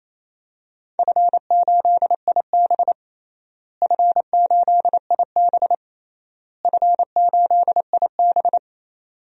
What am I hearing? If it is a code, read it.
Morse code, heard as a keyed tone, about 28 wpm, 700 Hz. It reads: F8IB F8IB F8IB
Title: DXCC-100 Call Signs - 4 Characters - Repeated 3x - 28wpm